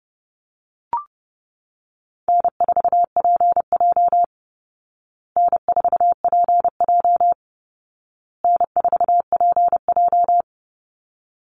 Morse code, heard as a keyed tone, about 30 wpm, 700 Hz. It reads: N4PJ N4PJ N4PJ
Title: Easy Call Signs Repeated 3x - 30wpm